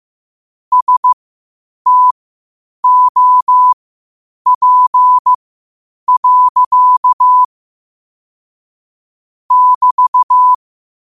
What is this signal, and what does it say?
Morse code, heard as a keyed tone, about 15 wpm, spaced as for 8 wpm, 1000 Hz. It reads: STOP. <BT>
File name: qotdfiles / 8wpm